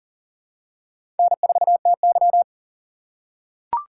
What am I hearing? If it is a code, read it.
Morse code, heard as a keyed tone, about 39 wpm, 700 Hz. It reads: N4TY
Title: USA Call Signs - 3x Character Spacing - 40wpm